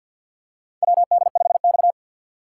Morse code, heard as a keyed tone, about 52 wpm, 700 Hz. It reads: WD5X
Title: USA Call Signs - 4 Characters - 6x Character Spacing - 50wpm